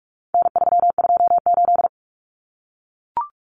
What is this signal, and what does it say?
Morse code, heard as a keyed tone, about 44 wpm, 700 Hz. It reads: N328
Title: Common Cut Numbers - 4-Digit 45wpm